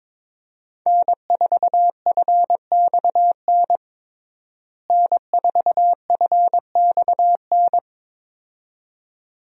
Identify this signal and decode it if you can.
Morse code, heard as a keyed tone, about 22 wpm, 700 Hz. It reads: N4FXN N4FXN
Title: USA Call Signs - 5 Characters - Repeated 2x - 22wpm